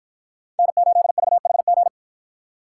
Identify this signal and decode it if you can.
Morse code, heard as a keyed tone, about 49 wpm, 700 Hz. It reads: N8VHG